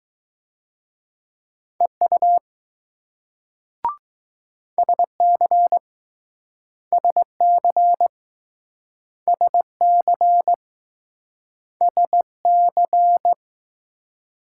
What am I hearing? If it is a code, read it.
Morse code, heard as a keyed tone, about 20 wpm, 700 Hz. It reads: EU SC SC SC SC